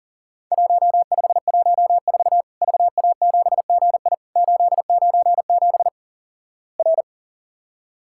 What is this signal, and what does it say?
Morse code, heard as a keyed tone, about 43 wpm, 700 Hz. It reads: 1514 VA7GI 897 R
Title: Worked All Europe DX Contest - QTC Exchange - 40wpm